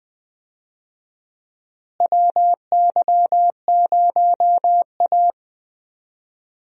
Morse code, heard as a keyed tone, about 20 wpm, 700 Hz. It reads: WY0A
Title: USA Call Signs - 4 Characters - Repeated 2x - 20wpm